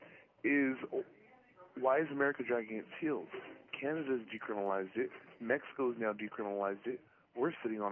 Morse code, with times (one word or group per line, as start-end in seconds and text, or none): none